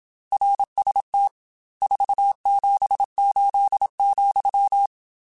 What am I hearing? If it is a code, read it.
Morse code, heard as a keyed tone, about 27 wpm, 800 Hz. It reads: RST 478,
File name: RSTs / RST-mp3s / Your RSTs @25